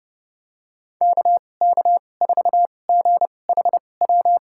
Morse code, heard as a keyed tone, about 30 wpm, spaced as for 22 wpm, 700 Hz. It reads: KK4GHW
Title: USA Call Signs - 2x Character Spacing - 30wpm